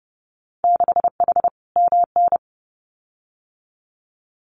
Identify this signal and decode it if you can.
Morse code, heard as a keyed tone, about 31 wpm, 700 Hz. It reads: 6H MN
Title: Winter Field Day Exchanges Encore 30wpm